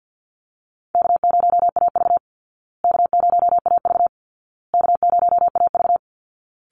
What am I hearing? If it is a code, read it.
Morse code, heard as a keyed tone, about 49 wpm, 700 Hz. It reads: K0AV K0AV K0AV
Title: Easy Call Signs Repeated 3x - 50wpm